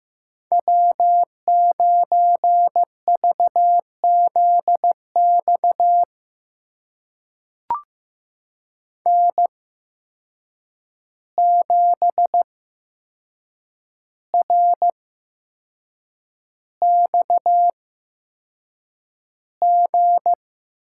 Morse code, heard as a keyed tone, about 15 wpm, spaced as for 6 wpm, 700 Hz. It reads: W9VZX N7RXG
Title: USA Call Signs - 5 Characters - 8x Character Spacing - 15wpm